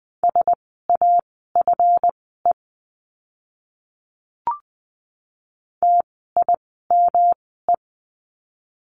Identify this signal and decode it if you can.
Morse code, heard as a keyed tone, about 20 wpm, spaced as for 14 wpm, 700 Hz. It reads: SAFE TIME
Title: 139 Lesson 23f - Letter f - Words 20wpm